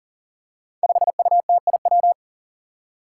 Morse code, heard as a keyed tone, about 40 wpm, 700 Hz. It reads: 5UTIW
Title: Five Challenging Letter-Number Combinations 40wpm